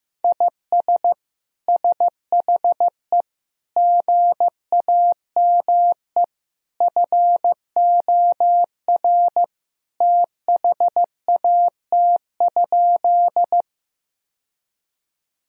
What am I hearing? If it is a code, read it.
Morse code, heard as a keyed tone, about 16 wpm, 700 Hz. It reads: IS SHE GAME FOR THAT?
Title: Sentences from Top 1000 Words - Review Part 3 - 15wpm